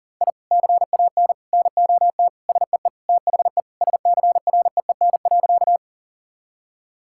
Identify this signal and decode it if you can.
Morse code, heard as a keyed tone, about 40 wpm, 700 Hz. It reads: I CAN NOT SEE THE SCREEN.
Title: Sentences from Top 1000 Words - Review Part 6 - 40wpm